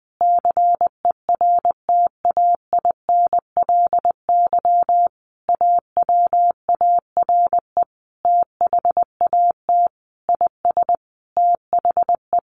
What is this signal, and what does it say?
Morse code, heard as a keyed tone, about 19 wpm, 700 Hz. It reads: CERTAINLY AWARE THAT IS THE
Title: Sentences from Top 1200 Words 20wpm